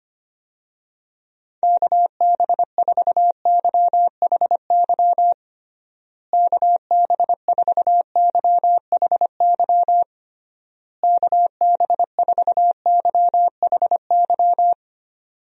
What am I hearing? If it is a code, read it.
Morse code, heard as a keyed tone, about 25 wpm, 700 Hz. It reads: KB4YHY KB4YHY KB4YHY
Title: USA Call Signs - 6 Characters - Repeated 3x - 25wpm